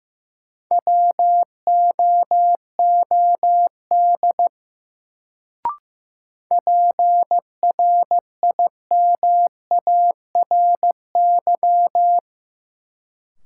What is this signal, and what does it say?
Morse code, heard as a keyed tone, about 15 wpm, 700 Hz. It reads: WOOD PRIMARY
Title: Sentences from Top 1100 Words - New Word Drill Alt - 15wpm